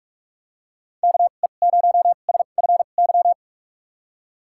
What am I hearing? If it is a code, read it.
Morse code, heard as a keyed tone, about 46 wpm, spaced as for 33 wpm, 700 Hz. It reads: KE0SFY
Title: USA Call Signs - 6 Characters - 2x Character Spacing - 45wpm